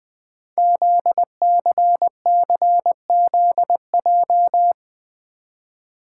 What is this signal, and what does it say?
Morse code, heard as a keyed tone, about 20 wpm, 700 Hz. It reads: ZCCZJ
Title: Code Groups - 5 Letters - 20wpm